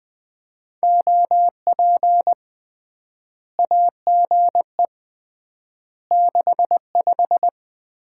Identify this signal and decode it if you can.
Morse code, heard as a keyed tone, about 20 wpm, 700 Hz. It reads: OP AGE 65
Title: Common QSO Phrases - 3x Word-Spacing - 20wpm